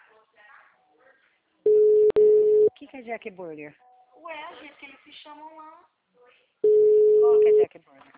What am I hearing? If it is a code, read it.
Ringback tone (Europe, 425 Hz), with speech.